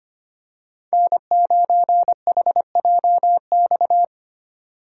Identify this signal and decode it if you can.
Morse code, heard as a keyed tone, about 29 wpm, 700 Hz. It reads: N9HJX
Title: Speed Racing - USA Call Signs 25wpm